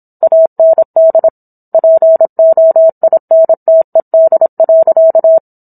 Morse code, heard as a keyed tone, about 27 wpm, 650 Hz. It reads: AND POINTED.